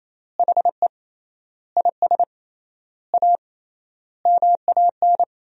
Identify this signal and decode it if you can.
Morse code, heard as a keyed tone, about 28 wpm, 700 Hz. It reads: HE IS A MAN
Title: 3x Word-Spacing - Sentences from Top 400 Words - 28wpm